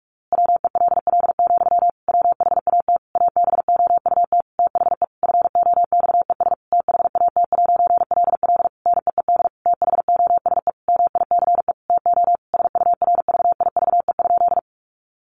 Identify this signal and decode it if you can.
Morse code, heard as a keyed tone, about 44 wpm, 700 Hz. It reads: WELL, WHAT ABOUT THE FOXES THAT'LL NEED THOSE MICE TO SURVIVE?
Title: A Sound Of Thunder 45wpm